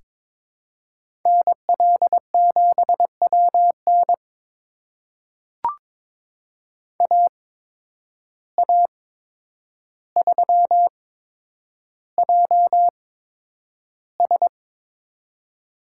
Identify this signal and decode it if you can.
Morse code, heard as a keyed tone, about 22 wpm, spaced as for 9 wpm, 700 Hz. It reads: NL7WN AA3JS
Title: USA Call Signs - 5 Characters - 8x Character Spacing - 22wpm